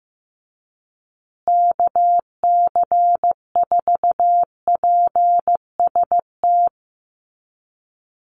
Morse code, heard as a keyed tone, about 15 wpm, 700 Hz. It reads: KC4PST
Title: USA Call Signs Repeated 3x 15wpm